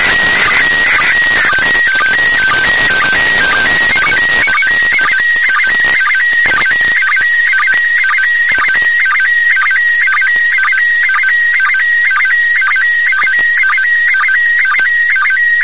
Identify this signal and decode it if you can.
SSTV (slow-scan television), PD 120.
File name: ISS-SSTV